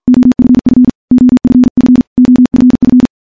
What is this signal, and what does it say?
Morse code, heard as a keyed tone, about 49 wpm, 250 Hz. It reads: OFF OFF OFF
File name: morse / morse-training-50WPM